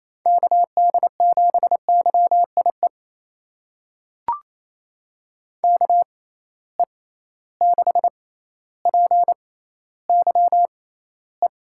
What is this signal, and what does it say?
Morse code, heard as a keyed tone, about 28 wpm, spaced as for 14 wpm, 700 Hz. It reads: KD7YIE KE6PYE
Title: USA Call Signs - 6 Characters - 6x Character Spacing - 28wpm